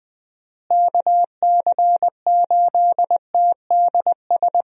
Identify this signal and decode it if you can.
Morse code, heard as a keyed tone, about 20 wpm, 700 Hz. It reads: KC8TDS
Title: USA Call Signs - 6 Characters - Repeated 3x - 20wpm